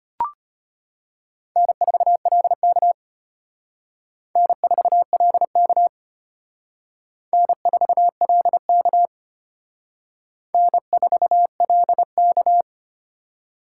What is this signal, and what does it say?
Morse code, heard as a keyed tone, about 32 wpm, 700 Hz. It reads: N4LK N4LK N4LK N4LK